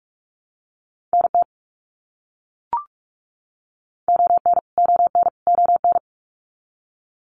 Morse code, heard as a keyed tone, about 44 wpm, 700 Hz. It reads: NT ON ON ON
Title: Canadian Provinces and Territories - Repeated 3x - 45wpm